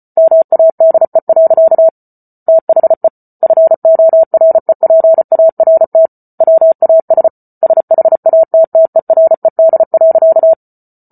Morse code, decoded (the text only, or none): MADE. THE FOREPART WAS SHATTERED.